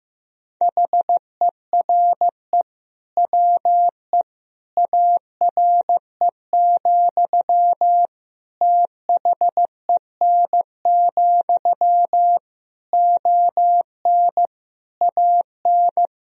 Morse code, heard as a keyed tone, about 15 wpm, 700 Hz. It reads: HERE WE ARE, THEN, ON AN
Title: Sentences from Top 1000 Words - Review Part 3 - 15wpm